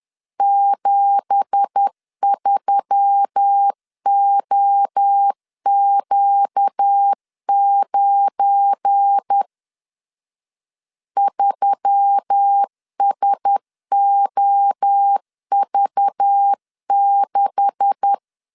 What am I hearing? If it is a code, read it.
Morse code, heard as a keyed tone, about 10 wpm, 800 Hz. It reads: 73OQ9 3SOV6
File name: LEZIONI / 9 2867 10wpm 2733